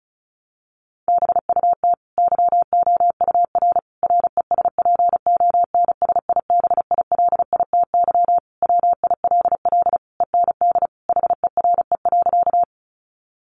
Morse code, heard as a keyed tone, about 34 wpm, 700 Hz. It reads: BUT YOUR RESPONSIBILITY WILL END HERE.
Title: Sentences from Top 1200 Words 35wpm